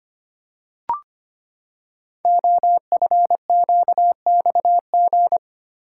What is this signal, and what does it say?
Morse code, heard as a keyed tone, about 25 wpm, 700 Hz. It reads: OFQXG
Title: Code Groups - 5 Letters - 25wpm